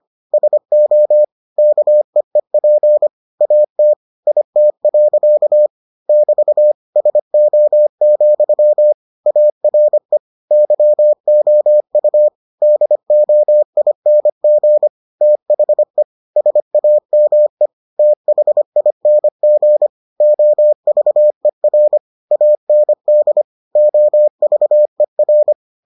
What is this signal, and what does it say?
Morse code, heard as a keyed tone, about 25 wpm, 600 Hz. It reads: SO KEEP AT IT. <BT> SO, ARE YOU DOING THE SAME THING OVER AND OVER